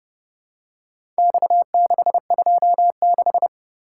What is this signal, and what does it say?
Morse code, heard as a keyed tone, about 30 wpm, 700 Hz. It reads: X626